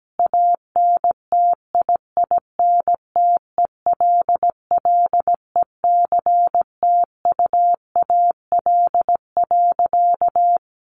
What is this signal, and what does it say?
Morse code, heard as a keyed tone, about 17 wpm, 700 Hz. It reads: ANTIINTELLECTUAL.
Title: A Sound Of Thunder 17wpm